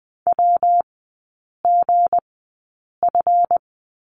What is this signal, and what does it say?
Morse code, heard as a keyed tone, about 20 wpm, spaced as for 8 wpm, 690 Hz. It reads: WGF
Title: 172 Lesson 28e - Number 9 - Three Characters 20wpm